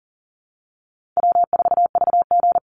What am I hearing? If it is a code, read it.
Morse code, heard as a keyed tone, about 41 wpm, 700 Hz. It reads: W4VG
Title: Easy Call Signs - 40wpm